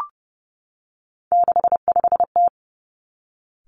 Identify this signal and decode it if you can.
Morse code, heard as a keyed tone, about 30 wpm, 700 Hz. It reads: 65T
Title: Three Challenging Letter-Number Combinations 30wpm